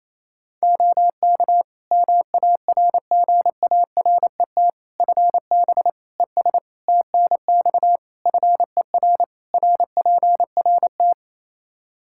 Morse code, heard as a keyed tone, about 28 wpm, 700 Hz. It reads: OK MARGARET FB ES TNX FER RPRT
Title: Practice QSO - Exchange 2 - Info - 28wpm